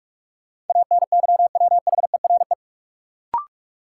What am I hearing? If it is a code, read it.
Morse code, heard as a keyed tone, about 46 wpm, 700 Hz. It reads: ANYWHERE
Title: Sentences from Top 1400 Words - New Word Drill - 45wpm